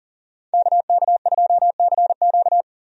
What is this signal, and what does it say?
Morse code, heard as a keyed tone, about 40 wpm, 700 Hz. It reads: KK2CQ
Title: USA Call Signs - 5 Characters - 40wpm